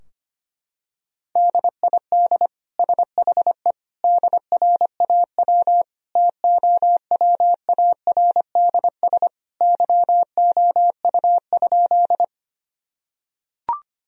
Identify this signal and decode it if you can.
Morse code, heard as a keyed tone, about 25 wpm, 700 Hz. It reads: DID SHE DRAW TOWARDS YOU?
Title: Sentences from Top 1000 Words - Review Part 2 - 25wpm